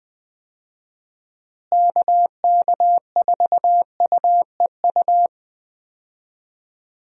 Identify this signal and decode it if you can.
Morse code, heard as a keyed tone, about 20 wpm, 700 Hz. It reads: KK4UEU
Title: USA Call Signs - 6 Characters - Repeated 2x - 20wpm